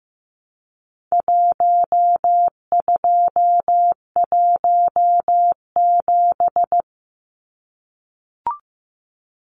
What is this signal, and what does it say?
Morse code, heard as a keyed tone, about 15 wpm, 700 Hz. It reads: 1217
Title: Numbers - Contest Style - 4-Digit 15wpm